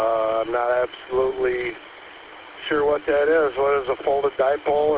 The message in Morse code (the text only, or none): none